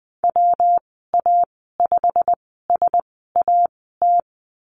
Morse code, heard as a keyed tone, about 20 wpm, spaced as for 15 wpm, 700 Hz. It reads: WA5SAT
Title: 122 Lesson 20g - Number 3 - Call Signs 20wpm